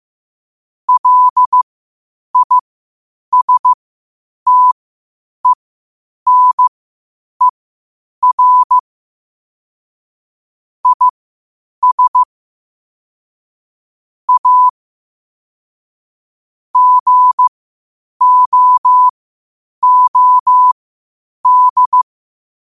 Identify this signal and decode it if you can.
Morse code, heard as a keyed tone, about 15 wpm, spaced as for 8 wpm, 1000 Hz. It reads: LISTENER IS A GOOD
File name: qotdfiles / 8wpm